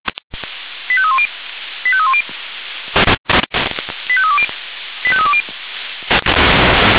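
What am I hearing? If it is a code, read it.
Five-tone selective call (ZVEI), heard as quick runs of five tones, tones of about 70 ms. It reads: 85310 85310 85320 85320
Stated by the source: ZVEI five-tone selective call received on SDR.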